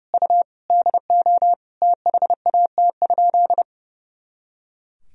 Morse code, heard as a keyed tone, about 30 wpm, 700 Hz. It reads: U DO THAT?
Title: Sentences from Top 100 Words and Abbreviations 30wpm